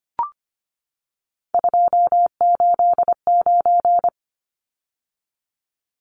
Morse code, heard as a keyed tone, about 25 wpm, 690 Hz. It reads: 289
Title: Numbers - 3-Digit 25wpm